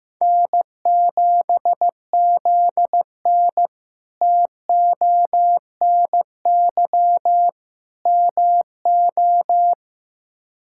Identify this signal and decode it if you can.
Morse code, heard as a keyed tone, about 15 wpm, 700 Hz. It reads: N7ZN TONY MO
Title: K1USN SST QSO Practice - 15wpm